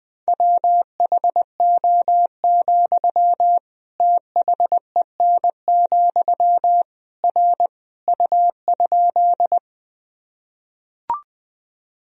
Morse code, heard as a keyed tone, about 20 wpm, 700 Hz. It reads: WHO, THEN, R U?
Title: Sentences from Top 100 Words and Abbreviations 20wpm